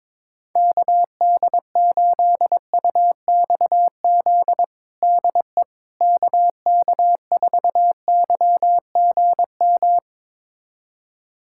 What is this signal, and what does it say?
Morse code, heard as a keyed tone, about 22 wpm, 700 Hz. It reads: KD8UXZ DE KK4YGM
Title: USA Callsign DE Callsign - 6 Characters - 22wpm